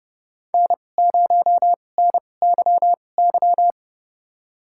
Morse code, heard as a keyed tone, about 30 wpm, spaced as for 22 wpm, 700 Hz. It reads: N0NYY
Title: USA Call Signs - 5 Characters - 2x Character Spacing - 30wpm